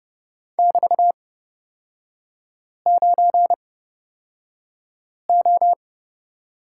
Morse code, heard as a keyed tone, about 30 wpm, spaced as for 5 wpm, 700 Hz. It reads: <BT>9O
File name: Lesson 36e - BT - Three Characters 30wpm